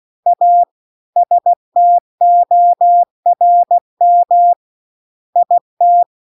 Morse code, heard as a keyed tone, about 16 wpm, 700 Hz. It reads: A STORM IT